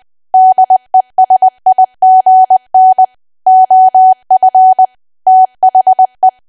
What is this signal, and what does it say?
Morse code, heard as a keyed tone, about 20 wpm, 750 Hz. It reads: DESIGN OF THE
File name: cwa / W1AW-Level3 / 20WPM